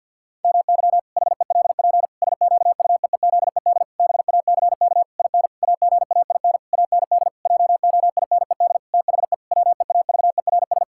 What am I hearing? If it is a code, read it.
Morse code, heard as a keyed tone, about 53 wpm, 700 Hz. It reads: MY HELP SQUEEZED BACK IN AGAIN AND JOINED THE WEAVERS
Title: Pangram Challenge v2 - 50wpm